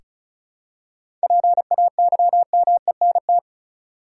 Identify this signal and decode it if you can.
Morse code, heard as a keyed tone, about 34 wpm, 700 Hz. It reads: PAYMENT
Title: Seven Letter Words - 35wpm